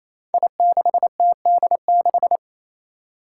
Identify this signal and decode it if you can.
Morse code, heard as a keyed tone, about 27 wpm, 700 Hz. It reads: I6TD6
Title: Five Challenging Letter-Number Combinations 28wpm